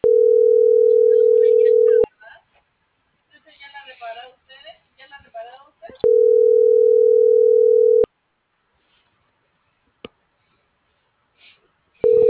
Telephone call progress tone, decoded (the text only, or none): ringback tone